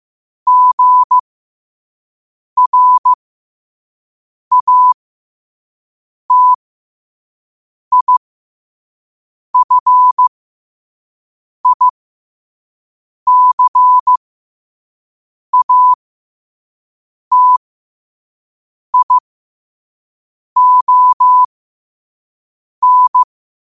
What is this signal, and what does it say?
Morse code, heard as a keyed tone, about 15 wpm, spaced as for 5 wpm, 1000 Hz. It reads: GRATIFICATION
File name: qotdfiles / 5wpm